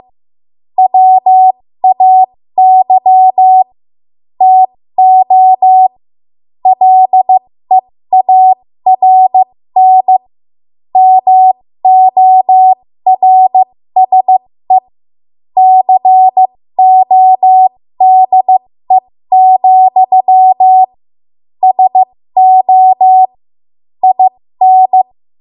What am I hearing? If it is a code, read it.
Morse code, heard as a keyed tone, about 15 wpm, spaced as for 13 wpm, 750 Hz. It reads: WAY TO LEARN MORSE CODE, SO IN
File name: Morse / 13WPM